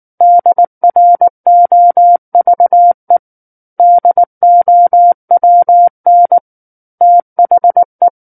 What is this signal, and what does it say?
Morse code, heard as a keyed tone, about 19 wpm, 700 Hz. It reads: DROVE DOWN THE
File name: Book One / 10-Chapter 10 - 19 WPM